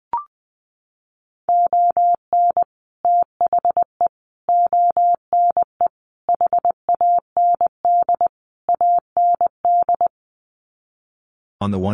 Morse code, heard as a keyed tone, about 20 wpm, 700 Hz. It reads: ON THE ONE HAND AND
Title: Sets of 5 Words Encore2 20wpm